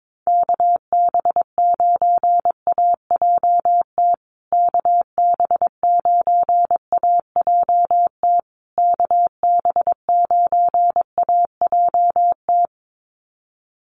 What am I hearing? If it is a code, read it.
Morse code, heard as a keyed tone, about 22 wpm, 700 Hz. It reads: KB9AJT KB9AJT KB9AJT
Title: USA Call Signs Repeated 3x 22wpm